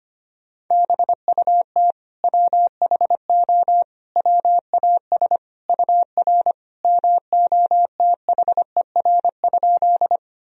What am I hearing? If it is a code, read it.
Morse code, heard as a keyed tone, about 25 wpm, 700 Hz. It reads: BUT WHO WAS UR MOTHER?
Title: Sentences from Top 200 Words and Abbreviations 25wpm